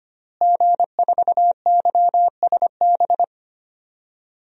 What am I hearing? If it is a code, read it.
Morse code, heard as a keyed tone, about 25 wpm, 700 Hz. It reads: G4YSB
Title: DXCC-100 Call Signs - 5 Characters - Repeated 3x - 25wpm